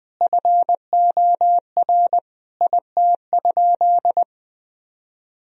Morse code, heard as a keyed tone, about 21 wpm, 700 Hz. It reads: FOR IT?
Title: Sentences from Top 1000 Words - Review Part 4 - 20wpm